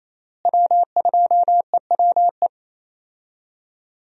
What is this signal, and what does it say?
Morse code, heard as a keyed tone, about 28 wpm, 700 Hz. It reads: W2EWE
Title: USA Call Signs - 5 Characters - Repeated 3x - 28wpm